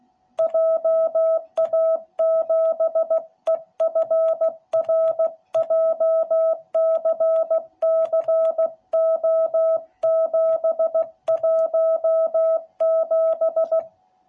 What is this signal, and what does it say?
Morse code, heard as a keyed tone, about 16 wpm, 660 Hz. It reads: JA7EFR JCCO717